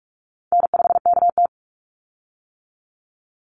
Morse code, heard as a keyed tone, about 44 wpm, 700 Hz. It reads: N5KT
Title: Easy Call Signs - 45wpm